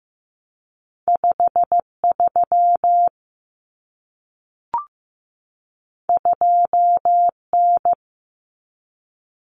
Morse code, heard as a keyed tone, about 15 wpm, 700 Hz. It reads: 53 2N
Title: Common Cut Numbers - 2-Digit 15wpm